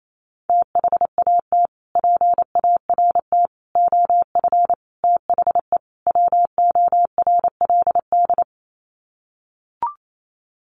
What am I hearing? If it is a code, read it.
Morse code, heard as a keyed tone, about 28 wpm, 700 Hz. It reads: THAT PART OF THE WORLD
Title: Sets of 5 Words Encore2 28wpm